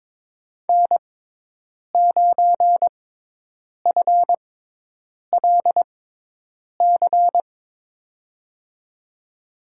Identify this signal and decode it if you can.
Morse code, heard as a keyed tone, about 22 wpm, spaced as for 8 wpm, 700 Hz. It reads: N9FLC
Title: USA Call Signs - 5 Characters - 6x Character Spacing - 22wpm